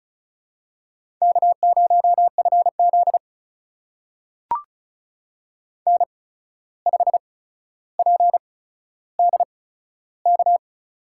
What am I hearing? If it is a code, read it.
Morse code, heard as a keyed tone, about 34 wpm, spaced as for 14 wpm, 700 Hz. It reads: K0FZ N5PDK
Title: USA Call Signs - 8x Character Spacing - 35wpm